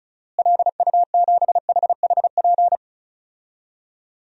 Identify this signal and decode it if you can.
Morse code, heard as a keyed tone, about 36 wpm, 700 Hz. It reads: LU7HHP
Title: DXCC-100 Call Signs - 6 Characters - Repeated 2x - 35wpm